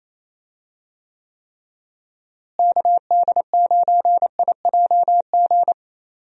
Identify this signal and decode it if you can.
Morse code, heard as a keyed tone, about 28 wpm, 700 Hz. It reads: KD9IJG
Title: USA Call Signs - 6 Characters - Repeated 3x - 28wpm